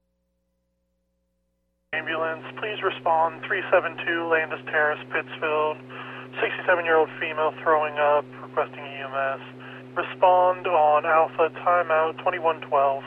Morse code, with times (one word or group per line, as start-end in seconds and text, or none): none